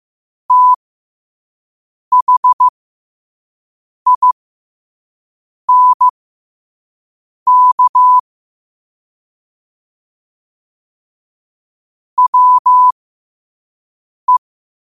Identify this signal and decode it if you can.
Morse code, heard as a keyed tone, about 15 wpm, spaced as for 5 wpm, 1000 Hz. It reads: THINK WE
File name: qotdfiles / 5wpm